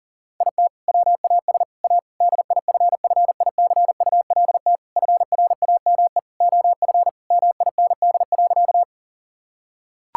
Morse code, heard as a keyed tone, about 40 wpm, 700 Hz. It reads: IT WAS A DIFFICULT FRAME OF MIND.
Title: Sentences from Top 1000 Words - Review Part 6 - 40wpm